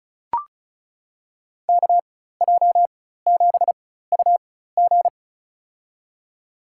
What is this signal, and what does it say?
Morse code, heard as a keyed tone, about 36 wpm, spaced as for 16 wpm, 700 Hz. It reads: KJ7UG
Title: USA Call Signs - 5 Characters - 4x Character Spacing - 35wpm